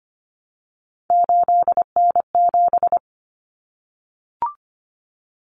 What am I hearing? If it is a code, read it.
Morse code, heard as a keyed tone, about 25 wpm, 700 Hz. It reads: 8N7
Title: Common Cut Numbers - 3-Digit 25wpm